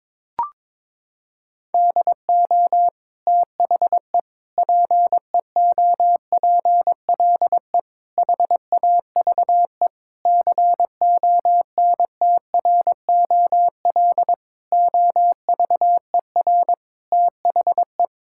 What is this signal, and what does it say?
Morse code, heard as a keyed tone, about 22 wpm, 700 Hz. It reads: DO THE PEOPLE HAVE CONTROL OVER THE
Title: Sentences from Top 1000 Words - Review Part 5 - 22wpm